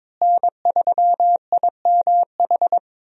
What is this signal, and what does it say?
Morse code, heard as a keyed tone, about 22 wpm, 700 Hz. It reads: N3IMH
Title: USA Call Signs - 5 Characters - Repeated 2x - 22wpm